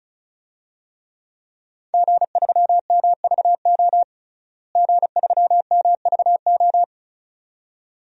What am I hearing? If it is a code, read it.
Morse code, heard as a keyed tone, about 36 wpm, 700 Hz. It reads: G3MVO G3MVO
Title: DXCC-100 Call Signs - 5 Characters - Repeated 2x - 35wpm